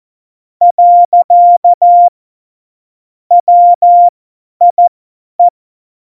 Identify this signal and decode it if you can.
Morse code, heard as a keyed tone, about 14 wpm, spaced as for 10 wpm, 700 Hz. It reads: . WIE
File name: Morsekurs / Witze / Witz 21 April 2020, 50 F70